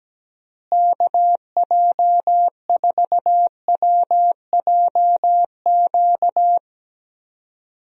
Morse code, heard as a keyed tone, about 17 wpm, 700 Hz. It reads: KJ4WJQ
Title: USA Call Signs - 6 Characters - Repeated 2x - 17wpm